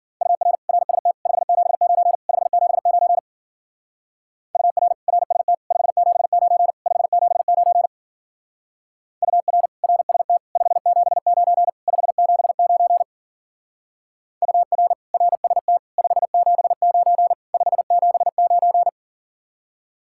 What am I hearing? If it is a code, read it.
Morse code, heard as a keyed tone, about 50 wpm, 700 Hz. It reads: UR RST 579 579 UR RST 579 579 UR RST 579 579 UR RST 579 579